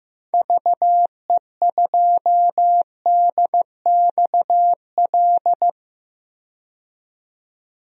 Morse code, heard as a keyed tone, about 15 wpm, 700 Hz. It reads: VE2DXL
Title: DXCC-100 Call Signs - 6 Characters - 15wpm